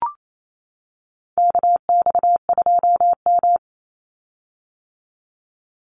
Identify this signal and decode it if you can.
Morse code, heard as a keyed tone, about 28 wpm, 700 Hz. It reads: KX2M
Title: USA Call Signs 28wpm